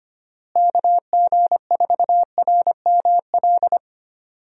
Morse code, heard as a keyed tone, about 25 wpm, 700 Hz. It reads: KG4RML
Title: USA Call Signs - 6 Characters - Repeated 2x - 25wpm